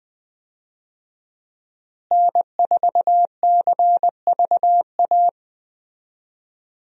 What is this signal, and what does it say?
Morse code, heard as a keyed tone, about 20 wpm, 700 Hz. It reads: N4CVA
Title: USA Call Signs - 5 Characters - Repeated 3x - 20wpm